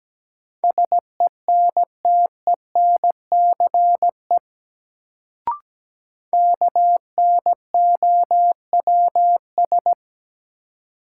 Morse code, heard as a keyed tone, about 17 wpm, 700 Hz. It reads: SENTENCE KNOWS
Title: Sentences from Top 1000 Words - Review Part 1 - New Word Drill - 17wpm